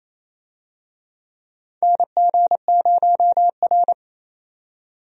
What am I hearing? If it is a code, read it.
Morse code, heard as a keyed tone, about 28 wpm, 700 Hz. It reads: NG0R
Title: USA Call Signs - 4 Characters - 28wpm